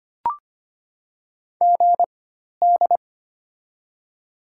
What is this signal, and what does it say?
Morse code, heard as a keyed tone, about 25 wpm, spaced as for 12 wpm, 700 Hz. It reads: GD
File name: Top 90 QSO Elements - 4x Character Spacing - 25wpm